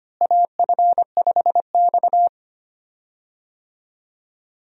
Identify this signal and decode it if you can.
Morse code, heard as a keyed tone, about 25 wpm, 700 Hz. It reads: AF5X
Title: USA Call Signs - 4 Characters - Repeated 3x - 25wpm